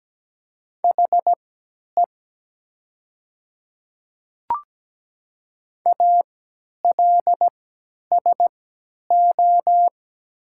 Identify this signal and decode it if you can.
Morse code, heard as a keyed tone, about 17 wpm, spaced as for 10 wpm, 700 Hz. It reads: HE ALSO
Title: 3x Character Spacing - Top 200 Words - 17wpm